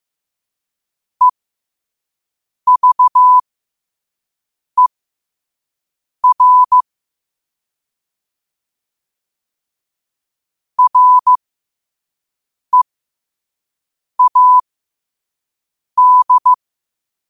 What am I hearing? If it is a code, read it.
Morse code, heard as a keyed tone, about 15 wpm, spaced as for 5 wpm, 1000 Hz. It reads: EVER READ
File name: qotdfiles / 5wpm